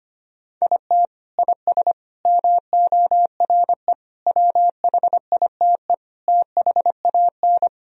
Morse code, heard as a keyed tone, about 25 wpm, 700 Hz. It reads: IT IS MORE WHITE THAN
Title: Sentences from Top 1000 Words - Review Part 1 - 25wpm